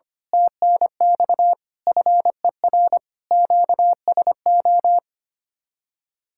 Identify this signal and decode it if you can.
Morse code, heard as a keyed tone, about 26 wpm, 700 Hz. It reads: TNX FER QSO
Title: Speed-Racing Common QSO Phrases Encore - 25wpm